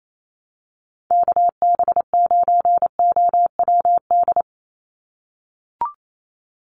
Morse code, heard as a keyed tone, about 28 wpm, 700 Hz. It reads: KB9OWD
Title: Call Signs - 28wpm